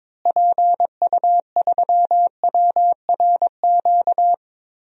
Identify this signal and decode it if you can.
Morse code, heard as a keyed tone, about 22 wpm, 700 Hz. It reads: PU3WRQ
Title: DXCC-100 Call Signs - 6 Characters - Repeated 2x - 22wpm